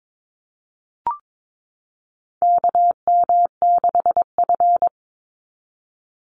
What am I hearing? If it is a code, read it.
Morse code, heard as a keyed tone, about 22 wpm, 700 Hz. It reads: KM6F
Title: USA Call Signs 22wpm